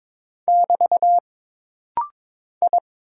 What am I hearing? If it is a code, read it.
Morse code, heard as a keyed tone, about 22 wpm, 700 Hz. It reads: <BT> I
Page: This rapid-fire timing is geared to push your ICR (Instant Character Recognition) skills.